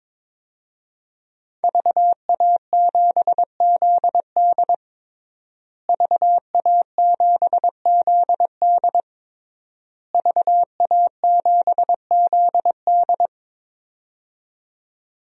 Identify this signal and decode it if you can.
Morse code, heard as a keyed tone, about 22 wpm, 700 Hz. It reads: VA7ZD VA7ZD VA7ZD
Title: DXCC-100 Call Signs - 5 Characters - Repeated 3x - 22wpm